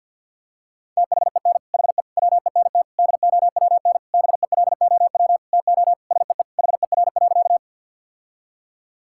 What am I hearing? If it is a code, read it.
Morse code, heard as a keyed tone, about 50 wpm, 690 Hz. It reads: THEN HE WENT DOWN BELOW TO SEE HER.
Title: Sentences from Top 1000 Words - Review Part 5 - 50wpm